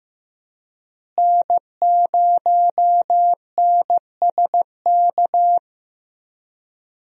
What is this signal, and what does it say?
Morse code, heard as a keyed tone, about 15 wpm, 700 Hz. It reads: N0NSK
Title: USA Call Signs - 5 Characters - Repeated 2x - 15wpm